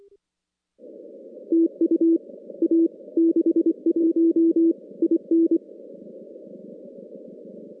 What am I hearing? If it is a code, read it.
Morse code, heard as a keyed tone, about 25 wpm, 330 Hz. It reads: TU A 61 IN
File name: contests / CW-SS-k5tr- -2100